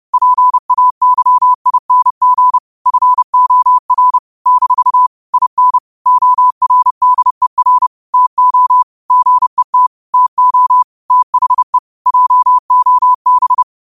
Morse code, heard as a keyed tone, about 30 wpm, 1000 Hz. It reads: PAYING FOR - IN ORDER TO GET TO THE JOB
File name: qotdfiles / 30wpm